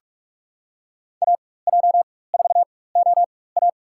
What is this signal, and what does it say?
Morse code, heard as a keyed tone, about 46 wpm, spaced as for 21 wpm, 700 Hz. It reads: AJ4OA
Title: USA Call Signs - 5 Characters - 4x Character Spacing - 45wpm